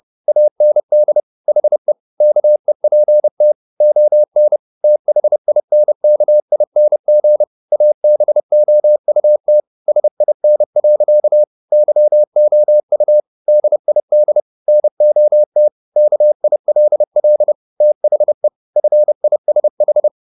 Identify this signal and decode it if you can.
Morse code, heard as a keyed tone, about 30 wpm, 600 Hz. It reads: AND HE KEPT ON THINKING ABOUT SIN. YOU DID NOT KILL THE FISH